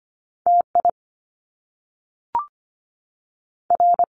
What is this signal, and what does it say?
Morse code, heard as a keyed tone, about 25 wpm, 700 Hz. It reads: TI R
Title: Top Two Letter Combinations in Words 25wpm